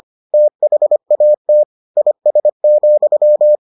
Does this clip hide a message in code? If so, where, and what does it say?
Yes: Morse code from 0.3 to 3.6 s, heard as a keyed tone, about 25 wpm, 600 Hz. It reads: THAT IS,